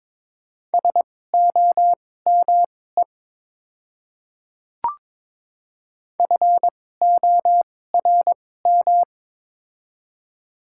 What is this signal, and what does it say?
Morse code, heard as a keyed tone, about 21 wpm, spaced as for 16 wpm, 700 Hz. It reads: SOME FORM
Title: 2x Character Spacing - Top 300 Words - 22wpm